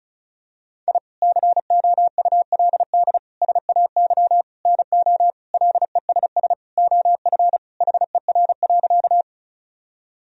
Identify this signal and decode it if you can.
Morse code, heard as a keyed tone, about 34 wpm, 700 Hz. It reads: I COULD SAY NO LESS OF HER.
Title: Sentences from Top 1000 Words - Review Part 2 - 35wpm